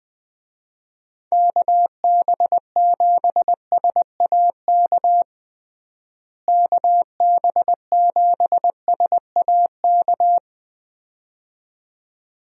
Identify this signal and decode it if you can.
Morse code, heard as a keyed tone, about 20 wpm, 700 Hz. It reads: KB7SAK KB7SAK
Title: USA Call Signs - 6 Characters - Repeated 2x - 20wpm